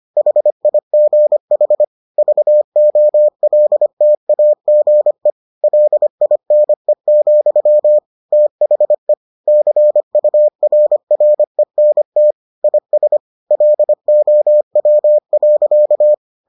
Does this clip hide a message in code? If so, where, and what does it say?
Yes: Morse code from 0.2 to 16.1 s, heard as a keyed tone, about 25 wpm, 600 Hz. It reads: HIGH VOLTAGE LINE, THE CURRENT IS LOW.